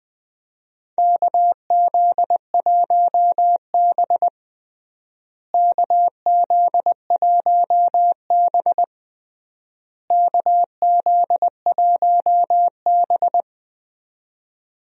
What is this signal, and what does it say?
Morse code, heard as a keyed tone, about 20 wpm, 700 Hz. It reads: KZ1B KZ1B KZ1B
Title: USA Call Signs - 4 Characters - Repeated 3x - 20wpm